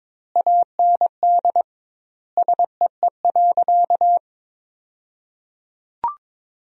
Morse code, heard as a keyed tone, about 22 wpm, 700 Hz. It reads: AND SEE.
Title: 2x Word-Spacing - Sentences from Top 100 Words - 22wpm